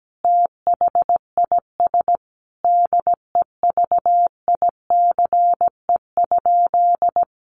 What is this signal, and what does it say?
Morse code, heard as a keyed tone, about 17 wpm, 700 Hz. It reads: THIS DEVICE?
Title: Sentences from Top 1500 Words 17wpm